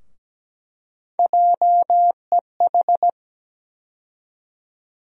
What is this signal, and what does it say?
Morse code, heard as a keyed tone, about 17 wpm, 700 Hz. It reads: JEH
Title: USA Call Sign Postfixes - 3 Letters 17wpm